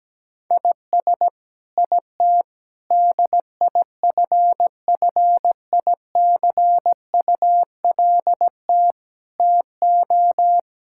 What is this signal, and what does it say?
Morse code, heard as a keyed tone, about 17 wpm, 700 Hz. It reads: IS IT DIFFICULT TO
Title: Sentences from Top 1000 Words - Review Part 6 - 17wpm